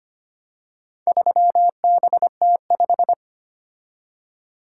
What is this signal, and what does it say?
Morse code, heard as a keyed tone, about 25 wpm, 700 Hz. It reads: 3BT5
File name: practice / 25wpm / Four Letter-Number Combinations - 25wpm